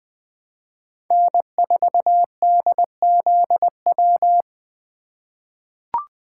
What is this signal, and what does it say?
Morse code, heard as a keyed tone, about 21 wpm, 700 Hz. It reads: N4DZW
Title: USA Call Signs - 5 Characters - 2x Character Spacing - 20wpm